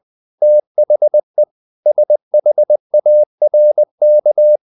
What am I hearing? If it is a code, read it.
Morse code, heard as a keyed tone, about 20 wpm, 600 Hz. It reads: THE SHARK
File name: the old man and the sea part 2 20wpm